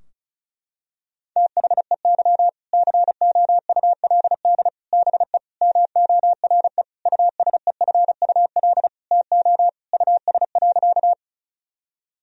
Morse code, heard as a keyed tone, about 36 wpm, 700 Hz. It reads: THEY COULD BE MORE USEFUL TO US.
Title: Sentences from Top 1000 Words - Review Part 4 - 35wpm